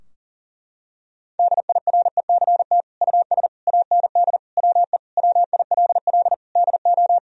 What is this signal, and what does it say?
Morse code, heard as a keyed tone, about 40 wpm, 700 Hz. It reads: DIRECT US AND WE WILL DO
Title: Sentences from Top 1000 Words - Review Part 2 - 40wpm